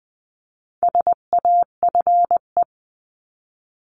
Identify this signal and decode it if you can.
Morse code, heard as a keyed tone, about 20 wpm, 700 Hz. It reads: SAFE
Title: 259 Lesson 42d - Words 18-20wpm